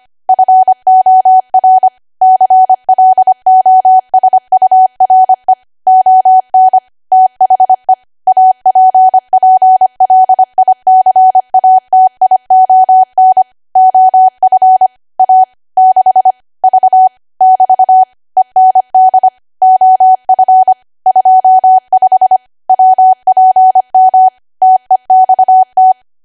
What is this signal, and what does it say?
Morse code, heard as a keyed tone, about 25 wpm, 750 Hz. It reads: FOR CLOSURE ON THE APPLICATION OF A 6 V <BT> END OF 25 WPM TEXT
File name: cwa / W1AW-Level3 / 25WPM